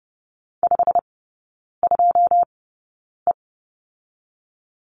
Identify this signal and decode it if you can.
Morse code, heard as a keyed tone, about 30 wpm, spaced as for 9 wpm, 700 Hz. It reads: 52E
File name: Lesson 19e - Letter w - Three Characters 30wpm